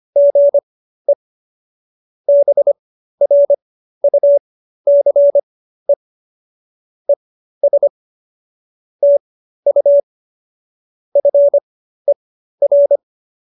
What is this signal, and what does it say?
Morse code, heard as a keyed tone, about 25 wpm, spaced as for 13 wpm, 570 Hz. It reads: GE BRUCE ES TU FER